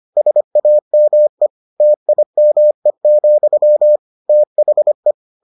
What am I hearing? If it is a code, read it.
Morse code, heard as a keyed tone, about 25 wpm, 600 Hz. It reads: SAME TIME, THE